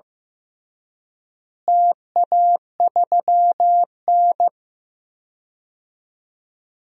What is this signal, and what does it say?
Morse code, heard as a keyed tone, about 15 wpm, 700 Hz. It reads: TA3N
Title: DXCC-100 Call Signs - 4 Characters - Repeated 2x - 15wpm